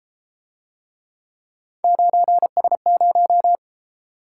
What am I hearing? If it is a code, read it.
Morse code, heard as a keyed tone, about 30 wpm, 700 Hz. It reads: 9S0